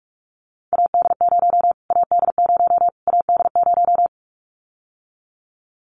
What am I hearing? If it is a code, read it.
Morse code, heard as a keyed tone, about 44 wpm, 700 Hz. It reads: AD0 AD0 AD0
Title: Callsign Prefixes for Top 25 DXCC - Repeated 3x - 45wpm